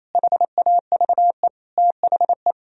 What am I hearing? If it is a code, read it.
Morse code, heard as a keyed tone, about 34 wpm, 700 Hz. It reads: HAVE THE
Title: Speed-Racing - Sets of 4 Words 28wpm